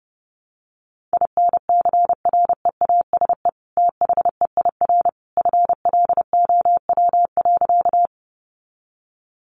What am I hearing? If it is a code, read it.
Morse code, heard as a keyed tone, about 30 wpm, 700 Hz. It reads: INCREASE THEIR FLOW.
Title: Sentences from Top 1200 Words 30wpm